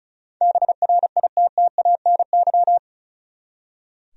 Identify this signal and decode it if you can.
Morse code, heard as a keyed tone, about 34 wpm, 700 Hz. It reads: BRITTANY
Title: USA Common Names - 35wpm